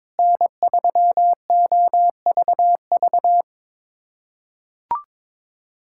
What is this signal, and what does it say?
Morse code, heard as a keyed tone, about 21 wpm, 700 Hz. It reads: N3OVV
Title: USA Callsign DE Callsign - 5 Characters - 22wpm